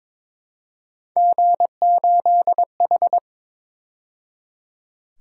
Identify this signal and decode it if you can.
Morse code, heard as a keyed tone, about 26 wpm, 700 Hz. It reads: G8H
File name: Speed-Racing - Three Letter-Number Combinations - 22wpm